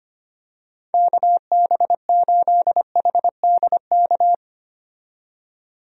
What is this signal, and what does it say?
Morse code, heard as a keyed tone, about 25 wpm, 700 Hz. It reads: KB8HDK
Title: USA Call Signs - 6 Characters - 25wpm